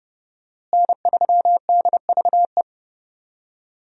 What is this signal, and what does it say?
Morse code, heard as a keyed tone, about 32 wpm, 700 Hz. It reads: N3DVE
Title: USA Call Signs - 5 Characters - Repeated 3x - 30wpm